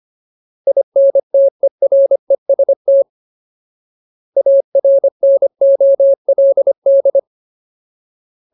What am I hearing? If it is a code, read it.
Morse code, heard as a keyed tone, about 25 wpm, 550 Hz. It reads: INTEREST ARNOLD
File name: Ultimate Word List / 25WPM Ultimate